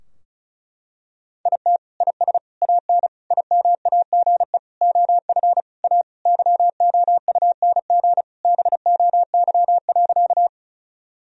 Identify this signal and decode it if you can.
Morse code, heard as a keyed tone, about 36 wpm, 700 Hz. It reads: IT IS AN IMAGE OF A YOUNG BOY.
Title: Sentences from Top 1000 Words - Review Part 4 - 35wpm